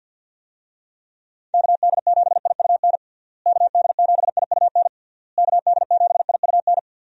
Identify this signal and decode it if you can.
Morse code, heard as a keyed tone, about 51 wpm, 700 Hz. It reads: KD7IUN KD7IUN KD7IUN
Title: USA Call Signs - 6 Characters - Repeated 3x - 50wpm